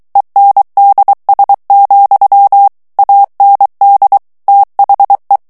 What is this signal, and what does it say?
Morse code, heard as a keyed tone, about 23 wpm, 800 Hz. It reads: ENDS, AND THE
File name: ANGLAIS texte clair 3 vitesse 22 mots minute